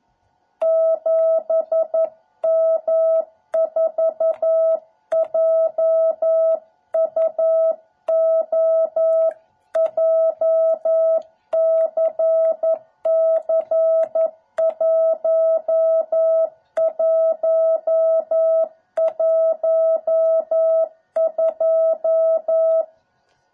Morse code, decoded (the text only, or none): 7M4JUO JCC1112